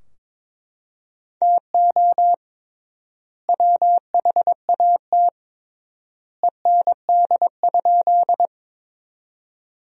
Morse code, heard as a keyed tone, about 22 wpm, 700 Hz. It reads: TO WHAT END?
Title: 3x Word-Spacing - Sentences from Top 200 Words - 22wpm